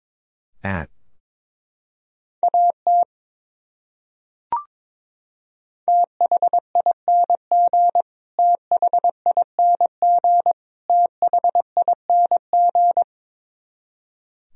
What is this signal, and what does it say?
Morse code, heard as a keyed tone, about 22 wpm, 700 Hz. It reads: AT THING THING THING
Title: Top 100 Words Repeated 22wpm